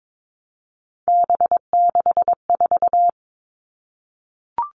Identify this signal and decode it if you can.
Morse code, heard as a keyed tone, about 22 wpm, 700 Hz. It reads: B64
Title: All Cut Numbers - 3-Digit 22wpm